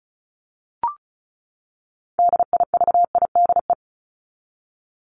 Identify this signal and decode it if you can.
Morse code, heard as a keyed tone, about 34 wpm, 700 Hz. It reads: DIVIDE
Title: Top 1501-1750 Words 35wpm